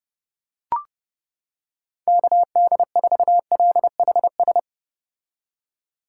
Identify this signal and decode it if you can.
Morse code, heard as a keyed tone, about 31 wpm, 700 Hz. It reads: KD4LHS